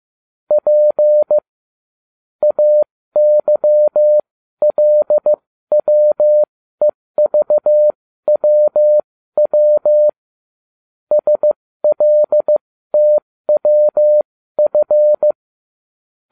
Morse code, heard as a keyed tone, about 15 wpm, spaced as for 13 wpm, 600 Hz. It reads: P AYLWEVWW SLTWF